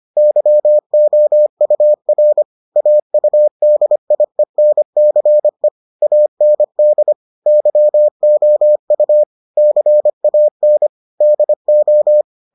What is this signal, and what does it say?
Morse code, heard as a keyed tone, about 25 wpm, 600 Hz. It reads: YOUR AUDIENCE AND YOU CAN DO